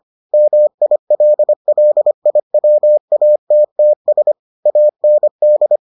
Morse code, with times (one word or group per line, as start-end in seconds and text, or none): 0.3-4.3 MILLIWATTS
4.7-5.8 AND